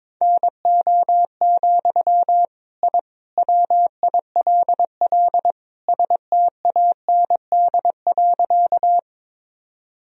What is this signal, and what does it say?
Morse code, heard as a keyed tone, about 22 wpm, 700 Hz. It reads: NO, I WILL STAND.
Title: Sentences from Top 1000 Words - Review Part 1 - 22wpm